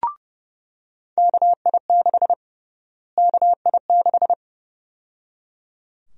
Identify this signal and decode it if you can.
Morse code, heard as a keyed tone, about 30 wpm, 700 Hz. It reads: KI6 KI6
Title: USA Call Sign Prefixes - Repeated 2x - 30wpm